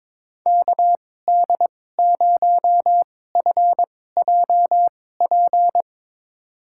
Morse code, heard as a keyed tone, about 22 wpm, spaced as for 16 wpm, 700 Hz. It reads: KD0FJP
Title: USA Call Signs - 6 Characters - 2x Character Spacing - 22wpm